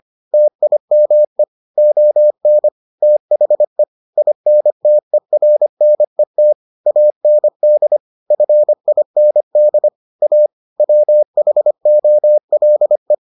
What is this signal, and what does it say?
Morse code, heard as a keyed tone, about 25 wpm, 600 Hz. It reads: TIME ON THE INTERNET AND FIND A WHOLE